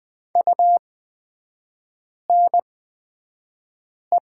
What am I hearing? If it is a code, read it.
Morse code, heard as a keyed tone, about 20 wpm, spaced as for 5 wpm, 700 Hz. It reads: UNE
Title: 224 Lesson 36e - BT - Three Characters 20wpm